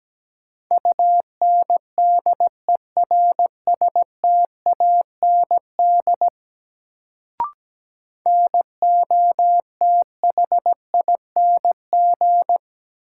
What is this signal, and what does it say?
Morse code, heard as a keyed tone, about 17 wpm, 700 Hz. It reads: UNDERSTAND NOTHING
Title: Sets of 4 Words - New Word Drill - 17wpm